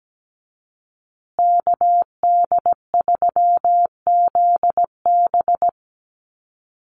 Morse code, heard as a keyed tone, about 17 wpm, 700 Hz. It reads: KD3ZB
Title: USA Call Signs Repeated 2x 17wpm